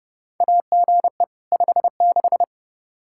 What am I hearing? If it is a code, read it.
Morse code, heard as a keyed tone, about 30 wpm, 700 Hz. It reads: AGE 56
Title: K1USN SST QSO Practice 30wpm